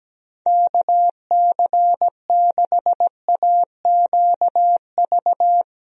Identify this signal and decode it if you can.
Morse code, heard as a keyed tone, about 17 wpm, 700 Hz. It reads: KC6AQV
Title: USA Call Signs - 6 Characters - Repeated 2x - 17wpm